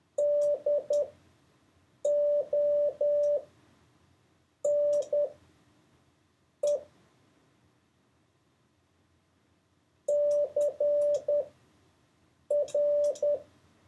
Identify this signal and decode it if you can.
Morse code, heard as a keyed tone, about 10 wpm, spaced as for 5 wpm, 580 Hz. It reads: DONE CR
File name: 4of6-words-1